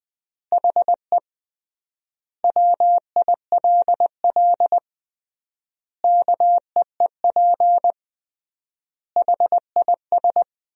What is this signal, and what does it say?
Morse code, heard as a keyed tone, about 20 wpm, 700 Hz. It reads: HE WILL KEEP HIS
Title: 3x Word-Spacing - Sentences from Top 300 Words - 20wpm